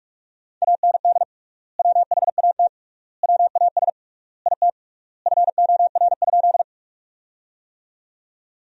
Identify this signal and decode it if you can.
Morse code, heard as a keyed tone, about 46 wpm, 700 Hz. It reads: AND WHAT WAS IT FOR?
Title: 3x Word-Spacing - Sentences from Top 100 Words - 45wpm